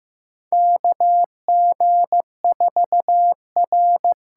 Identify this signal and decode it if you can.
Morse code, heard as a keyed tone, about 15 wpm, 700 Hz. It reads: KG4R
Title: USA Call Signs - 6 Characters - 8x Character Spacing - 15wpm